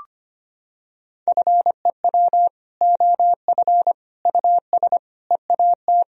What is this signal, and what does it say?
Morse code, heard as a keyed tone, about 25 wpm, 700 Hz. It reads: FEW OF US EAT
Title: Sentences from Top 1000 Words - Review Part 3 - 25wpm